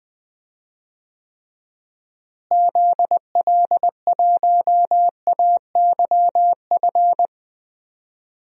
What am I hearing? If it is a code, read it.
Morse code, heard as a keyed tone, about 20 wpm, 700 Hz. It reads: ZL1AYF
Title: DXCC-100 Call Signs - 6 Characters - Repeated 3x - 20wpm